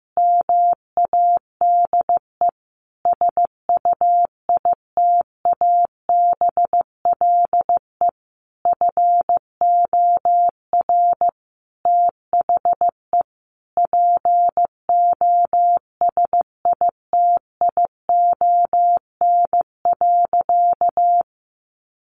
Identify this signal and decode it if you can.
Morse code, heard as a keyed tone, about 15 wpm, 700 Hz. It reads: MADE SUITABLE FOR THE POSITION.
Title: Sentences from Top 1400 Words 15wpm